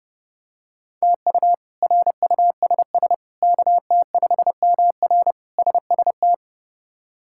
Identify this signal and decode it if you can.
Morse code, heard as a keyed tone, about 30 wpm, 700 Hz. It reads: TU RUSS KT5MR SST
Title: K1USN SST QSO Practice - 30wpm